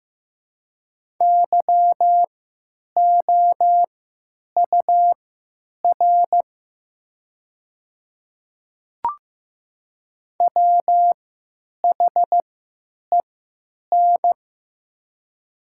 Morse code, heard as a keyed tone, about 15 wpm, spaced as for 9 wpm, 700 Hz. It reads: YOUR WHEN
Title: 3x Character Spacing - Top 100 Words - 15wpm